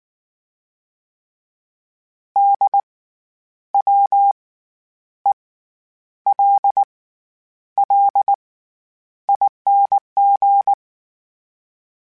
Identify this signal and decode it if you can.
Morse code, heard as a keyed tone, about 19 wpm, spaced as for 9 wpm, 800 Hz. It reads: DWELLING
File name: Special INGs / Special INGs-mp3s / SlowToFaster ing @18